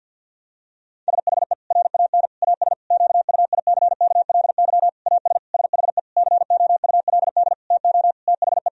Morse code, heard as a keyed tone, about 50 wpm, 700 Hz. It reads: SHE RAN AS QUICKLY AS SHE COULD TO THE
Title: Sentences from Top 1000 Words - Review Part 4 - 50wpm